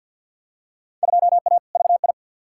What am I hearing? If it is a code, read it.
Morse code, heard as a keyed tone, about 53 wpm, 700 Hz. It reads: 2A VI
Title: Field Day Exchanges Encore - 50wpm